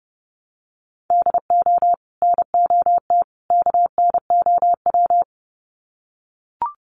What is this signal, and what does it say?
Morse code, heard as a keyed tone, about 30 wpm, 700 Hz. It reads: DO NOT KNOW
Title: Sets of 3 Words 30wpm